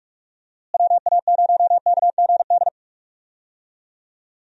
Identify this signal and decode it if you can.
Morse code, heard as a keyed tone, about 42 wpm, 700 Hz. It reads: WA0KGD
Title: USA Call Signs - 6 Characters - Repeated 2x - 45wpm